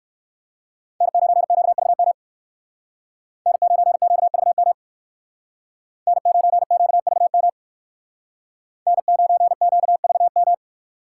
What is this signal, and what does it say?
Morse code, heard as a keyed tone, about 59 wpm, 700 Hz. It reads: N9QVM N9QVM N9QVM N9QVM